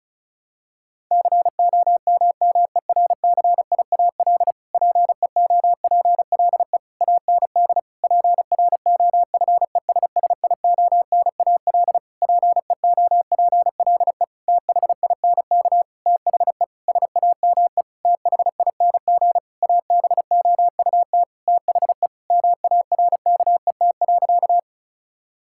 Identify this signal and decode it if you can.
Morse code, heard as a keyed tone, about 36 wpm, 700 Hz. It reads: COMMERCIAL PEOPLE AND PROFESSIONAL PEOPLE THINK THE SAME THING ABOUT THE MARKET.
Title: Sentences from Top 1000 Words - Review Part 5 - 35wpm